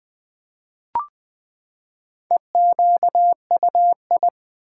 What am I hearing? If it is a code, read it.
Morse code, heard as a keyed tone, about 20 wpm, 700 Hz. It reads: EQUI
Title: Build A Word - 4 Letters At A Time - 10-Letter Words - 20wpm